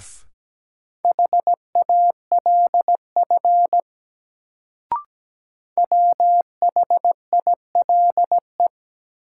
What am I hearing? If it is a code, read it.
Morse code, heard as a keyed tone, about 17 wpm, 700 Hz. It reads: HALF WHILE
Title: Sets of 3 Words - New Word Drill - 17wpm